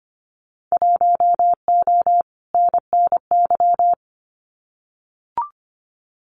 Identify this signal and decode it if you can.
Morse code, heard as a keyed tone, about 25 wpm, 700 Hz. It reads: 1O NNY
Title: Winter Field Day Exchanges 25wpm